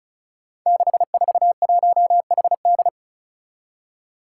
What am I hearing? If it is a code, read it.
Morse code, heard as a keyed tone, about 36 wpm, 700 Hz. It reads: 641HD
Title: Five Challenging Letter-Number Combinations 35wpm